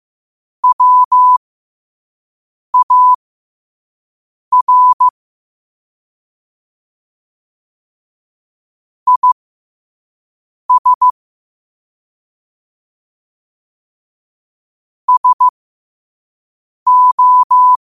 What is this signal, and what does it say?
Morse code, heard as a keyed tone, about 15 wpm, spaced as for 5 wpm, 1000 Hz. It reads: WAR IS SO
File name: qotdfiles / 5wpm